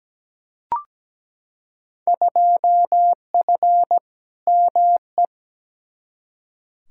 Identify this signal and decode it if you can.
Morse code, heard as a keyed tone, about 17 wpm, 700 Hz. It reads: 2F ME
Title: Field Day Exchanges - 17wpm